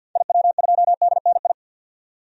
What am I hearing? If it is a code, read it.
Morse code, heard as a keyed tone, about 46 wpm, 700 Hz. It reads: IW2DNI
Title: DXCC-100 Call Signs - 6 Characters - 50wpm